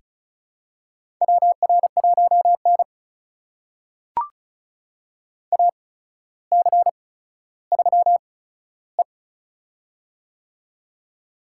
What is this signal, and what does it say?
Morse code, heard as a keyed tone, about 36 wpm, spaced as for 15 wpm, 700 Hz. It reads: WR1N AC3E
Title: USA Call Signs - 4 Characters - 8x Character Spacing - 35wpm